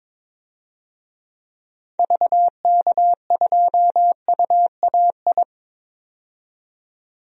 Morse code, heard as a keyed tone, about 22 wpm, 700 Hz. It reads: VK2UAI
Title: DXCC-100 Call Signs - 6 Characters - 22wpm